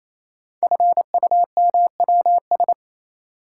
Morse code, heard as a keyed tone, about 28 wpm, 700 Hz. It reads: FUMWS
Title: Code Groups - 5 Letters - 28wpm